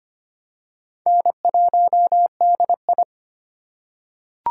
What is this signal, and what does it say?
Morse code, heard as a keyed tone, about 25 wpm, 700 Hz. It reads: N1DI
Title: USA Call Signs - 4 Characters - 2x Character Spacing - 25wpm